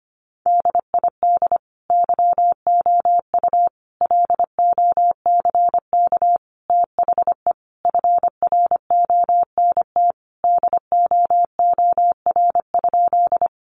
Morse code, heard as a keyed tone, about 25 wpm, 700 Hz. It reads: DID YOU LOCK THE FRONT DOOR?
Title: Sentences from Top 1300 Words 25wpm